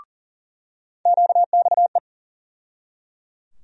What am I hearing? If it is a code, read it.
Morse code, heard as a keyed tone, about 40 wpm, 700 Hz. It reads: QXE
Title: USA Call Sign Postfixes - 3 Letters 40wpm